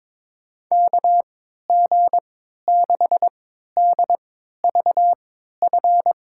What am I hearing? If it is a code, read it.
Morse code, heard as a keyed tone, about 22 wpm, spaced as for 12 wpm, 700 Hz. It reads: KG6DVF
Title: USA Call Signs - 6 Characters - 3x Character Spacing - 22wpm